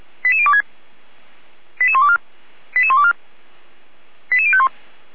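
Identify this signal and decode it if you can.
Five-tone selective call (ZVEI), heard as quick runs of five tones, tones of about 70 ms. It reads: 88026 88124 88124 88051
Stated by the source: ZVEI five-tone selective call received on SDR.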